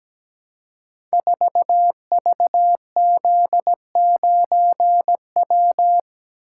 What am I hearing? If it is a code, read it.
Morse code, heard as a keyed tone, about 17 wpm, 700 Hz. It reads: 4VZ9W
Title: Code Groups - 5 Letter:Numbers - 17wpm